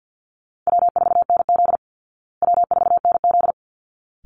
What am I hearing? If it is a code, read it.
Morse code, heard as a keyed tone, about 50 wpm, 700 Hz. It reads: W4NZ W4NZ
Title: Call Signs Repeated 3x - 50wpm